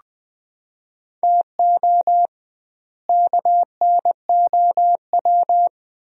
Morse code, heard as a keyed tone, about 20 wpm, 700 Hz. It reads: TO KNOW
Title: 2x Word-Spacing - Sentences from Top 400 Words - 20wpm